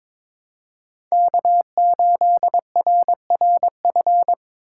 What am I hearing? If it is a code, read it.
Morse code, heard as a keyed tone, about 22 wpm, 700 Hz. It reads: K8RRF
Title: USA Call Signs - 5 Characters - 2x Character Spacing - 22wpm